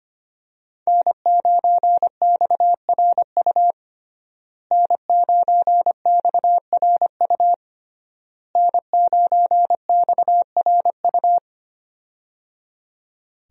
Morse code, heard as a keyed tone, about 25 wpm, 700 Hz. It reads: N9XRU N9XRU N9XRU
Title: USA Call Signs - 5 Characters - Repeated 3x - 25wpm